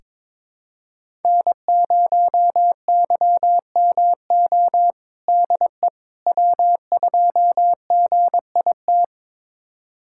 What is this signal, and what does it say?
Morse code, heard as a keyed tone, about 22 wpm, 700 Hz. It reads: N0YMO DE W2GIT
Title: USA Callsign DE Callsign - 5 Characters - 22wpm